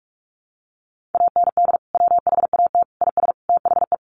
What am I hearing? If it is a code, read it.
Morse code, heard as a keyed tone, about 44 wpm, 700 Hz. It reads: AND WHAT IS THE
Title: Sentences from Top 1500 Words 45wpm